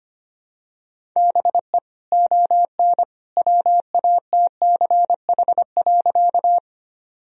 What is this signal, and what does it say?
Morse code, heard as a keyed tone, about 25 wpm, 700 Hz. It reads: BE ON WATCH.
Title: Sentences from Top 1000 Words - Review Part 1 - 25wpm